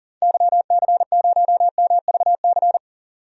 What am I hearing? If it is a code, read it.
Morse code, heard as a keyed tone, about 40 wpm, 700 Hz. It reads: YC0MVC
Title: DXCC-100 Call Signs - 6 Characters - 40wpm